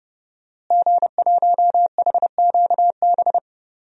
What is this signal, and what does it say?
Morse code, heard as a keyed tone, about 30 wpm, 700 Hz. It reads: G1HQB
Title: DXCC-100 Call Signs - 5 Characters - Repeated 3x - 30wpm